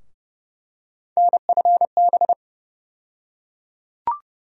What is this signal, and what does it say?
Morse code, heard as a keyed tone, about 30 wpm, 700 Hz. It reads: NFB
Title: USA Call Sign Postfixes - 3 Letters 30wpm